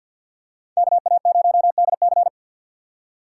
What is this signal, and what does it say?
Morse code, heard as a keyed tone, about 53 wpm, 700 Hz. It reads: KA0DC
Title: USA Call Signs - 3x Character Spacing - 50wpm